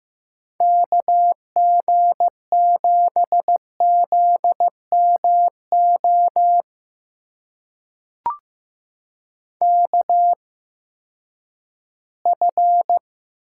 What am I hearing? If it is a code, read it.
Morse code, heard as a keyed tone, about 15 wpm, spaced as for 10 wpm, 700 Hz. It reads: KG7ZMO KF
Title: USA Call Signs - 6 Characters - 8x Character Spacing - 15wpm